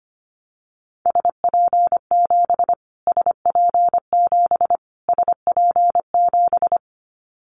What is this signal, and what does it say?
Morse code, heard as a keyed tone, about 25 wpm, 700 Hz. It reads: SP7 SP7 SP7
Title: Callsign Prefixes for Top 25 DXCC - Repeated 3x - 25wpm